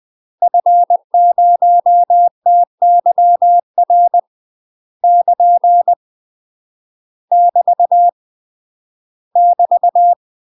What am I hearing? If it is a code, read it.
Morse code, heard as a keyed tone, about 20 wpm, 700 Hz. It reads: F0TYR <KN> <BT> <BT>
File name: UK MASTER QSOs TRAINING / 20WPM / UK MASTER QSOs 0007